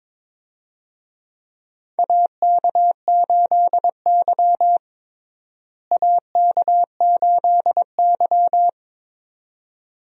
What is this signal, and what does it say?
Morse code, heard as a keyed tone, about 22 wpm, 700 Hz. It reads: AK8Y AK8Y
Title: USA Call Signs - 4 Characters - Repeated 2x - 22wpm